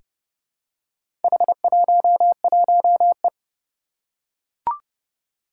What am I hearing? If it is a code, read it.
Morse code, heard as a keyed tone, about 30 wpm, 700 Hz. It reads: H11E
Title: Four Challenging Letter-Number Combinations 30wpm